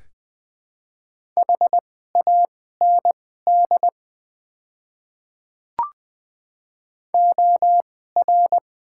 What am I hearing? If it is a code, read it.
Morse code, heard as a keyed tone, about 20 wpm, spaced as for 14 wpm, 700 Hz. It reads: HAND OR
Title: 2x Character Spacing - Top 200 Words - 20wpm